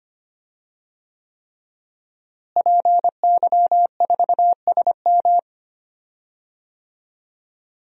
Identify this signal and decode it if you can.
Morse code, heard as a keyed tone, about 25 wpm, 700 Hz. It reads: PY4SM
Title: DXCC-100 Call Signs - 5 Characters - Repeated 3x - 25wpm